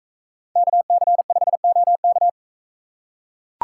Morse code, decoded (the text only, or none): KC5OK